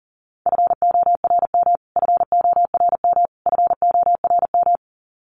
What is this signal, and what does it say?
Morse code, heard as a keyed tone, about 40 wpm, 700 Hz. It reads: FORM FORM FORM
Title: Top 500 Words Repeated 40wpm